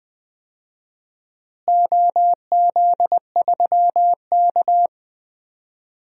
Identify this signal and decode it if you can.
Morse code, heard as a keyed tone, about 20 wpm, 700 Hz. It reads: OZ3K
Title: DXCC-100 Call Signs - 4 Characters - Repeated 3x - 20wpm